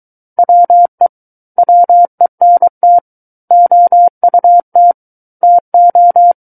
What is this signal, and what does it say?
Morse code, heard as a keyed tone, about 23 wpm, 700 Hz. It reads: WE WENT OUT TO
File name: Book One / 14-Chapter 14 - 23 WPM